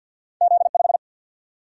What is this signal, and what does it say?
Morse code, heard as a keyed tone, about 53 wpm, 700 Hz. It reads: Z5
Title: Single-Letter Single-Number 50wpm